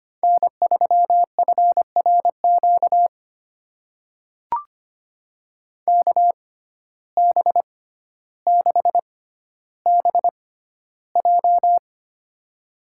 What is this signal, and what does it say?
Morse code, heard as a keyed tone, about 25 wpm, spaced as for 13 wpm, 700 Hz. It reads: N3FRQ KB6BJ
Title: USA Call Signs - 6x Character Spacing - 25wpm